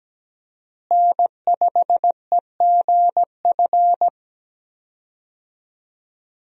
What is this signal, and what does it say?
Morse code, heard as a keyed tone, about 17 wpm, 700 Hz. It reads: N5EGF
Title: USA Call Signs - 5 Characters - Repeated 2x - 17wpm